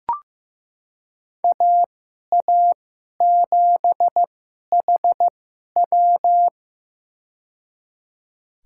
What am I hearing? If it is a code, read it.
Morse code, heard as a keyed tone, about 15 wpm, spaced as for 11 wpm, 700 Hz. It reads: AA7HW
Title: USA Call Signs - 5 Characters - 2x Character Spacing - 15wpm